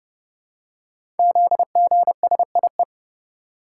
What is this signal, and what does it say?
Morse code, heard as a keyed tone, about 30 wpm, 700 Hz. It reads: ZGSIE
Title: Code Groups - 5 Letters - 30wpm